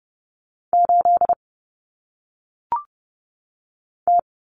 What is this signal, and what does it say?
Morse code, heard as a keyed tone, about 30 wpm, 700 Hz. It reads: 8 T
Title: Common Cut Numbers - 1-Digit 30wpm